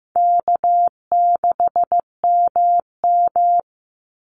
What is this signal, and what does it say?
Morse code, heard as a keyed tone, about 15 wpm, 700 Hz. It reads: K6MM
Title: Call Signs Repeated 2x - 15wpm